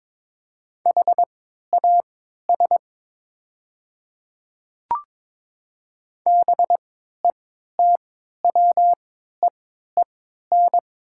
Morse code, heard as a keyed tone, about 23 wpm, spaced as for 13 wpm, 700 Hz. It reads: HAS BETWEEN
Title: 3x Character Spacing - Top 300 Words - 22wpm